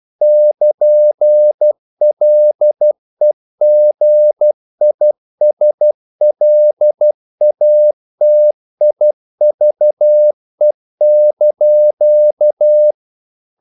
Morse code, heard as a keyed tone, about 12 wpm, 600 Hz. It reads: <KN>LEGISLATIVE)
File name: CW / Practice / NationalCapitals12